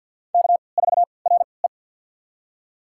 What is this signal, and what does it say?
Morse code, heard as a keyed tone, about 52 wpm, spaced as for 29 wpm, 700 Hz. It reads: K4RE
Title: USA Call Signs - 4 Characters - 3x Character Spacing - 50wpm